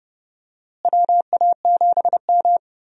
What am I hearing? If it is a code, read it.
Morse code, heard as a keyed tone, about 30 wpm, 700 Hz. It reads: WA7M
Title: USA Call Signs - 4 Characters - Repeated 2x - 30wpm